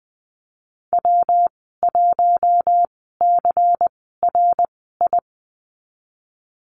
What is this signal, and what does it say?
Morse code, heard as a keyed tone, about 20 wpm, spaced as for 15 wpm, 700 Hz. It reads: W1CRI
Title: 173 Lesson 28g - Number 9 - Call Signs 20wpm